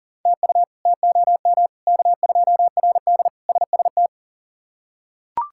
Morse code, heard as a keyed tone, about 38 wpm, 700 Hz. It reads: TU TOM K2RD SST
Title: K1USN SST QSO Practice - 40wpm